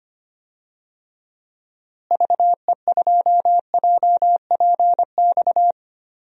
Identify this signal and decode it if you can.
Morse code, heard as a keyed tone, about 25 wpm, 700 Hz. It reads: VE2JPX
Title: DXCC-100 Call Signs - 6 Characters - Repeated 2x - 25wpm